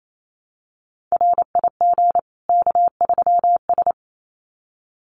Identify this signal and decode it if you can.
Morse code, heard as a keyed tone, about 28 wpm, 700 Hz. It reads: RIG K3S
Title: Common QSO Phrases 28wpm